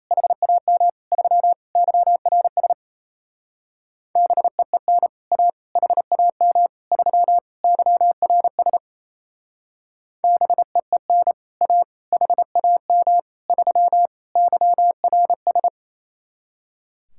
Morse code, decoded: HAM 3 YRS BEEN A HAM 3 YRS BEEN A HAM 3 YRS

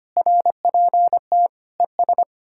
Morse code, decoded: RPT ES

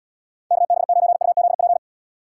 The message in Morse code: KB9UZP

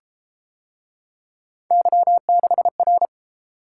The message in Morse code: Y6R

